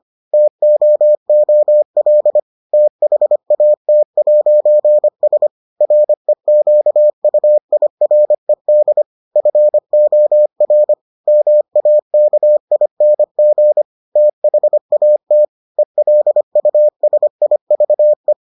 Morse code, heard as a keyed tone, about 25 wpm, 600 Hz. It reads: TOOL THAT'S REQUIRED FOR MAKING THAT ELUSIVE